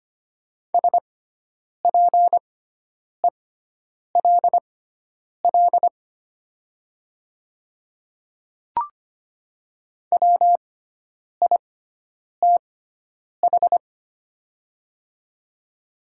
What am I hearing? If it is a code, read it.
Morse code, heard as a keyed tone, about 25 wpm, spaced as for 9 wpm, 700 Hz. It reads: SPELL WITH
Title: 6x Character Spacing - Top 200 Words - 25wpm